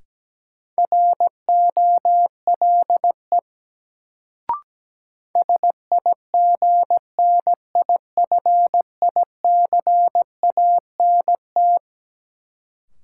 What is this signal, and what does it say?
Morse code, heard as a keyed tone, about 17 wpm, 700 Hz. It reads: ROLE SIGNIFICANT
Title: Sentences from Top 1000 Words - Review Part 6 - New Word Drill - 17wpm